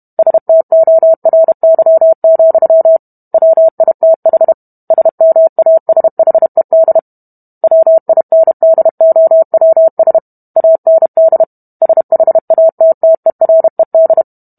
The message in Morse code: STORY, WITH SMASHED WINDOWS AND SHATTERED

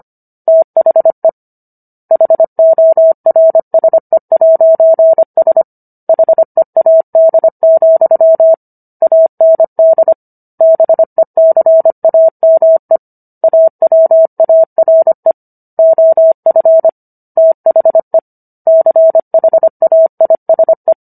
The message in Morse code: THE HORSE'S HEAD, AND BECAME AWARE OF THE CHAISE

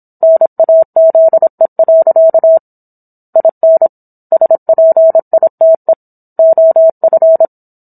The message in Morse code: NAZE. IN SPITE OF

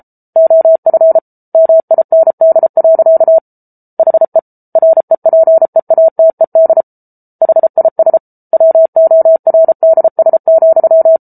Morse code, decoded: OF MIND. HE REPEATED HIS WORDS,

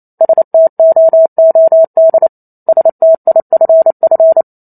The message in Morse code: STOOD STIFF

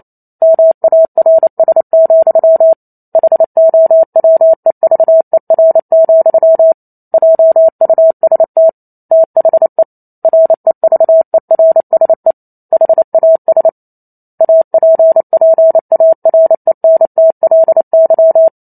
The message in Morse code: MARS, HOWEVER, JUST THE REVERSE HAS APPARENTLY

